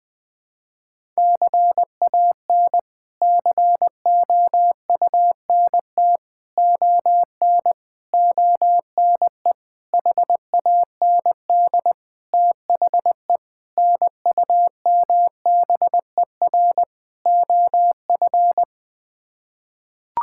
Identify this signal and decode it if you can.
Morse code, heard as a keyed tone, about 20 wpm, 700 Hz. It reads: CAN COUNT ON ONE HAND THE NUMBER OF
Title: Sets of 8 Words - 20wpm